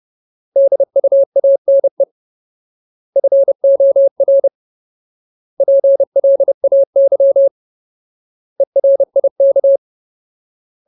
Morse code, heard as a keyed tone, about 30 wpm, 550 Hz. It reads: DUANE FOR PLAY ERIK
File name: Ultimate Word List / 30WPM Ultimate